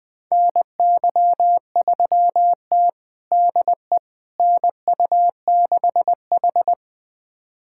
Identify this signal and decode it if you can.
Morse code, heard as a keyed tone, about 20 wpm, 700 Hz. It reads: NY3T DE NU6H
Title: USA Callsign DE Callsign - 4 Characters - 20wpm